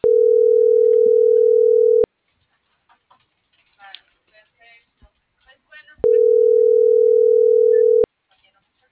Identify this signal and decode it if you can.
Ringback tone (North America), 440 + 480 Hz.